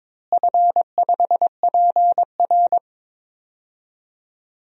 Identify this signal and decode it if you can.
Morse code, heard as a keyed tone, about 22 wpm, 700 Hz. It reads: F5PR
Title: DXCC-100 Call Signs - 4 Characters - Repeated 3x - 22wpm